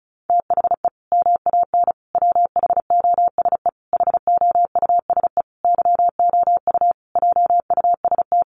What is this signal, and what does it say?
Morse code, heard as a keyed tone, about 34 wpm, 700 Hz. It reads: THE MAN WHOSE HOUSE YOU JUST
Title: Sentences from Top 1200 Words 35wpm